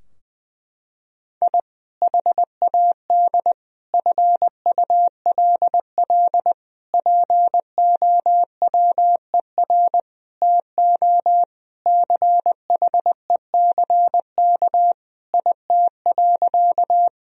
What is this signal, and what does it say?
Morse code, heard as a keyed tone, about 20 wpm, 700 Hz. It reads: I HAD FULL POWER TO CHECK IT.
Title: Sentences from Top 1000 Words - Review Part 3 - 20wpm